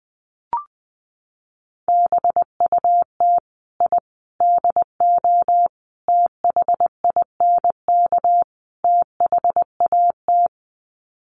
Morse code, heard as a keyed tone, about 20 wpm, 700 Hz. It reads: BUT I DO THINK THAT
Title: Sets of 5 Words Encore2 20wpm